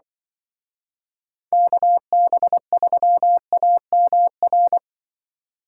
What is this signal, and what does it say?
Morse code, heard as a keyed tone, about 23 wpm, 700 Hz. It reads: KB3AMR
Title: Speed Racing - USA 2x3 Call Signs 20wpm